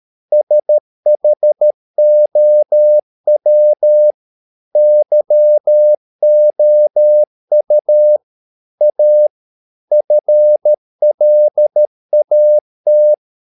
Morse code, heard as a keyed tone, about 13 wpm, 600 Hz. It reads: SHOW YOU A FLAT